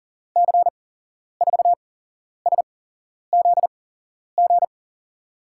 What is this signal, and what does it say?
Morse code, heard as a keyed tone, about 40 wpm, spaced as for 11 wpm, 700 Hz. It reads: C4SZG
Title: USA Call Signs - 6 Characters - 8x Character Spacing - 40wpm